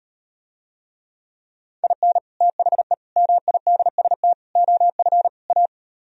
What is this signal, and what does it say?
Morse code, heard as a keyed tone, about 38 wpm, 700 Hz. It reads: IN THE MIDST OF A